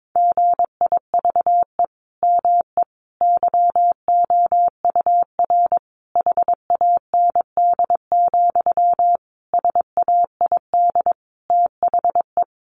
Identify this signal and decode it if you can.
Morse code, heard as a keyed tone, about 22 wpm, 700 Hz. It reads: GIVE ME YOUR HAND, SAID THE
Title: Sentences from Top 1500 Words 22wpm